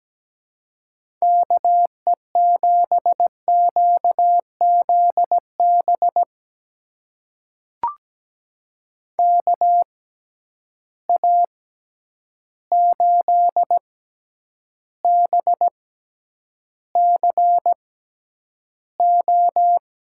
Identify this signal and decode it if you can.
Morse code, heard as a keyed tone, about 17 wpm, spaced as for 9 wpm, 700 Hz. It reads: KE7QZB KA8BCO
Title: USA Call Signs - 6 Characters - 6x Character Spacing - 17wpm